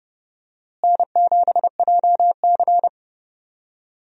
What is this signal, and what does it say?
Morse code, heard as a keyed tone, about 30 wpm, 700 Hz. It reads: N7JC
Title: USA Call Signs - 4 Characters - Repeated 2x - 30wpm